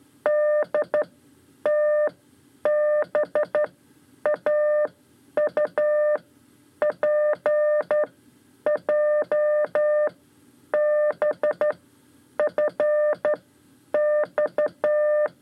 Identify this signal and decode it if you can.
Morse code, heard as a keyed tone, about 12 wpm, spaced as for 9 wpm, 600 Hz. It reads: DTBAU PJBFX
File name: 5 Lezione CW 2